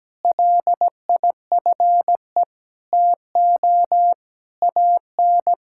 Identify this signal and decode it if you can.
Morse code, heard as a keyed tone, about 17 wpm, 700 Hz. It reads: LIFE TO AN
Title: Sentences from Top 1000 Words - Review Part 5 - 17wpm